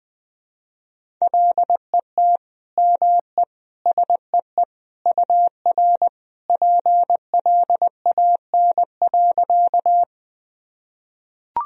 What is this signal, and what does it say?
Morse code, heard as a keyed tone, about 20 wpm, 700 Hz. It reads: LET ME SEE UR PLAN.
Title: Sentences from Top 500 Words and Abbreviations 20wpm